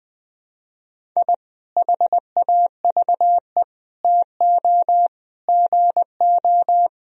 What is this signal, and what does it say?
Morse code, heard as a keyed tone, about 20 wpm, 700 Hz. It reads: I HAVE TO GO
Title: Sentences from Top 1000 Words - Review Part 1 - 20wpm